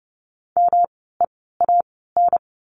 Morse code, heard as a keyed tone, about 30 wpm, spaced as for 17 wpm, 700 Hz. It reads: MEAN
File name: course / 30:10wpm / 237 Lesson 38f - Letter q - Words 30wpm